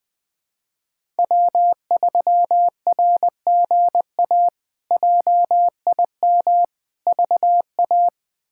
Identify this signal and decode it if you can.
Morse code, heard as a keyed tone, about 21 wpm, 700 Hz. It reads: W3RGA JIM VA
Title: K1USN SST QSO Practice - 20wpm